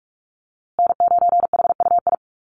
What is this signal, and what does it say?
Morse code, heard as a keyed tone, about 44 wpm, 700 Hz. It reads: N9HUI
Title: USA Call Signs Repeated 3x 45wpm